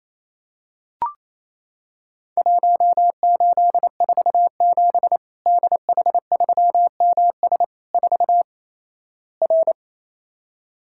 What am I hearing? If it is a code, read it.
Morse code, heard as a keyed tone, about 28 wpm, 700 Hz. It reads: 1847 DH3MS 4 R
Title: Worked All Europe DX Contest - QTC Exchange - 28wpm